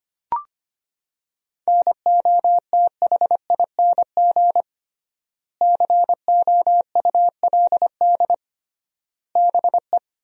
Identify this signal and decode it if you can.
Morse code, heard as a keyed tone, about 25 wpm, 700 Hz. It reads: NOTHING COULD BE
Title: 3x Word-Spacing - Sentences from Top 500 Words - 25wpm